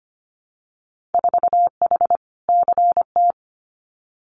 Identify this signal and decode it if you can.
Morse code, heard as a keyed tone, about 25 wpm, 700 Hz. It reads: 4H CT
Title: Winter Field Day Exchanges Encore 25wpm